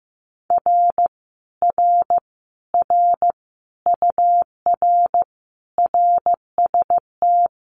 Morse code, heard as a keyed tone, about 16 wpm, 700 Hz. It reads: R R R UR RST
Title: Common QSO Phrases Encore 15wpm